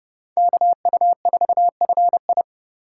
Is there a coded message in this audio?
Yes: Morse code from 0.4 to 2.4 s, heard as a keyed tone, about 30 wpm, 700 Hz. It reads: KU4FI